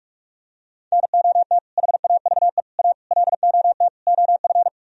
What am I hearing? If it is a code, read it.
Morse code, heard as a keyed tone, about 49 wpm, 700 Hz. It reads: NOT HAVE A LOT OF